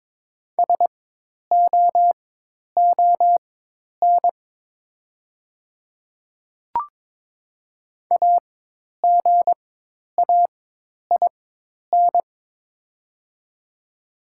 Morse code, heard as a keyed tone, about 23 wpm, spaced as for 10 wpm, 700 Hz. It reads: SOON AGAIN
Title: 4x Character Spacing - Top 500 Words - 22wpm